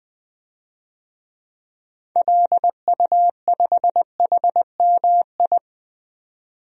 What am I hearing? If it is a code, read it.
Morse code, heard as a keyed tone, about 20 wpm, 700 Hz. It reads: LU5HMI
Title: DXCC-100 Call Signs - 6 Characters - 20wpm